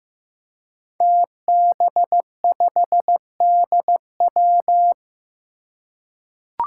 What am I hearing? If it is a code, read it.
Morse code, heard as a keyed tone, about 15 wpm, 700 Hz. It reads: TB5DW
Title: Five Challenging Letter-Number Combinations 15wpm